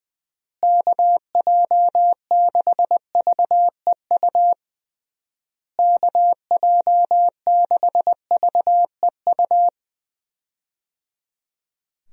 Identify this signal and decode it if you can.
Morse code, heard as a keyed tone, about 20 wpm, 700 Hz. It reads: KJ6VEU KJ6VEU
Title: USA Call Signs - 6 Characters - Repeated 3x - 20wpm